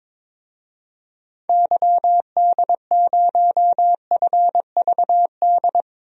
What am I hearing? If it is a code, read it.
Morse code, heard as a keyed tone, about 22 wpm, 700 Hz. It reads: YD0FVD
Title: DXCC-100 Call Signs - 6 Characters - 22wpm